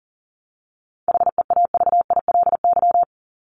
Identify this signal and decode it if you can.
Morse code, heard as a keyed tone, about 40 wpm, 700 Hz. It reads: HEAVILY
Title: LY Suffix - Top 100 Words with Base Words 40wpm